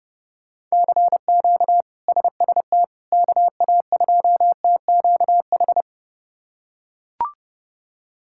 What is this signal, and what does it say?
Morse code, heard as a keyed tone, about 32 wpm, 700 Hz. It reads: CQ SST KA2TQH
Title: K1USN SST QSO Practice - 30wpm